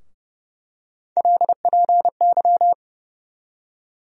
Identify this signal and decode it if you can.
Morse code, heard as a keyed tone, about 30 wpm, 700 Hz. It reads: LPY
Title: USA Call Sign Postfixes - 3 Letters 30wpm